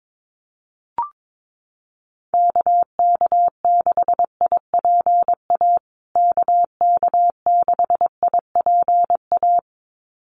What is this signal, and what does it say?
Morse code, heard as a keyed tone, about 22 wpm, 700 Hz. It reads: KK6IPA KK6IPA
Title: USA Call Signs Repeated 2x 22wpm